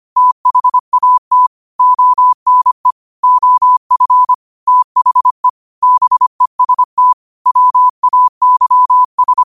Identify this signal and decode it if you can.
Morse code, heard as a keyed tone, about 25 wpm, 1000 Hz. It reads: THAT ONE OF THE BEST WAYS
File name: qotdfiles / 25wpm